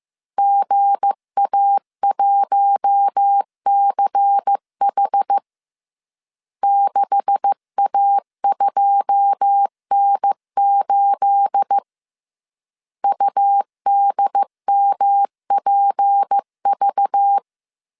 Morse code, heard as a keyed tone, about 15 wpm, 800 Hz. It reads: GA1CH 6A2N8 UDMPV
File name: CW / DET / 1 9 1500C S A 14wpm 3222